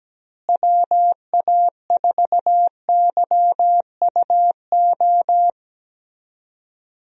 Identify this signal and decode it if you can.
Morse code, heard as a keyed tone, about 17 wpm, 700 Hz. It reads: WA4YUO
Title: USA Call Signs - 6 Characters - Repeated 2x - 17wpm